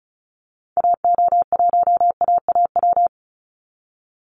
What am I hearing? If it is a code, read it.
Morse code, heard as a keyed tone, about 34 wpm, 700 Hz. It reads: AO1AAW
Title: Call Signs Repeated 3x - 35wpm